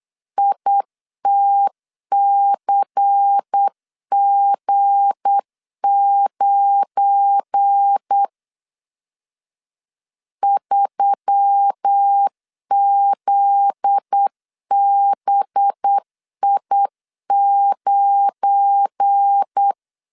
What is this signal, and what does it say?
Morse code, heard as a keyed tone, about 8 wpm, 800 Hz. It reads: ITCG9 3ZBI9
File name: LEZIONI / 8 YJ45 / 8 Y 8wpm 3245